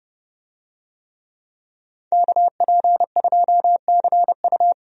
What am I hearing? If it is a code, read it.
Morse code, heard as a keyed tone, about 30 wpm, 700 Hz. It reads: KP2CU
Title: DXCC-100 Call Signs - 5 Characters - 30wpm